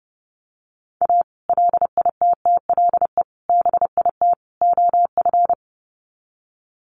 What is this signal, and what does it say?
Morse code, heard as a keyed tone, about 30 wpm, 700 Hz. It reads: A LITTLE BIT OF
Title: Sets of 4 Words 30wpm